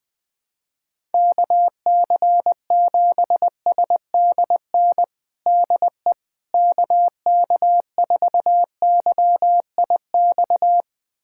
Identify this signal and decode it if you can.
Morse code, heard as a keyed tone, about 20 wpm, 700 Hz. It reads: KC7SDN DE KK4YIX
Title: USA Callsign DE Callsign - 6 Characters - 20wpm